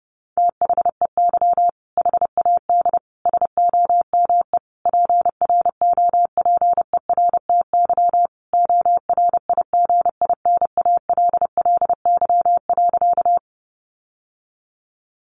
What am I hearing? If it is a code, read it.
Morse code, heard as a keyed tone, about 30 wpm, 700 Hz. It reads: THEY HAD SOME PROPERTY ORIGINALLY.
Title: Sentences from Top 1500 Words 30wpm